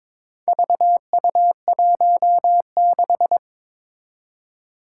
Morse code, heard as a keyed tone, about 22 wpm, 700 Hz. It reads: VU16
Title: Four Challenging Letter-Number Combinations 22wpm